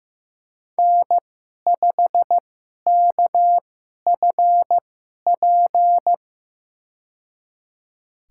Morse code, heard as a keyed tone, about 15 wpm, spaced as for 11 wpm, 700 Hz. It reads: N5KFP
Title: USA Call Signs - 5 Characters - 2x Character Spacing - 15wpm